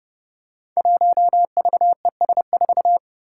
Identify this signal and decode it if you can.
Morse code, heard as a keyed tone, about 30 wpm, 700 Hz. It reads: 1VES4
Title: Five Challenging Letter-Number Combinations 30wpm